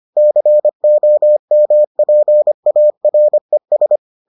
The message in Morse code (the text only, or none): COMPARES